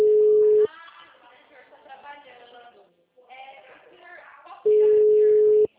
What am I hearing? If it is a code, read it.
Ringback tone (Europe, 425 Hz), with speech.